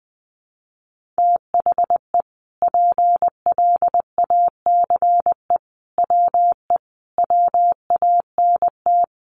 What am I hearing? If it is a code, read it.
Morse code, heard as a keyed tone, about 19 wpm, 700 Hz. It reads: THE PLACE WE WANT
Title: Sentences from Top 1400 Words 20wpm